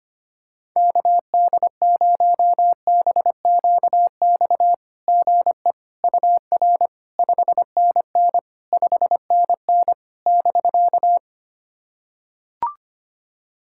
Morse code, heard as a keyed tone, about 25 wpm, 700 Hz. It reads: KD0BQX GE UR 5NN 5NN <BK>
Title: SOTA QSO Practice 25wpm